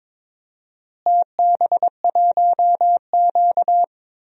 Morse code, heard as a keyed tone, about 22 wpm, 700 Hz. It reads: TB1Q
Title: DXCC-100 Call Signs - 4 Characters - Repeated 3x - 22wpm